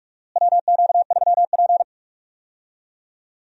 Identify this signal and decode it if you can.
Morse code, heard as a keyed tone, about 46 wpm, 700 Hz. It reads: WQ3P
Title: USA Call Signs - 4 Characters - Repeated 3x - 45wpm